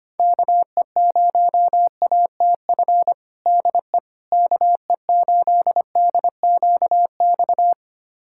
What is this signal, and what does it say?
Morse code, heard as a keyed tone, about 25 wpm, 700 Hz. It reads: KE0ATF DE KE8DQX
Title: USA Callsign DE Callsign 25wpm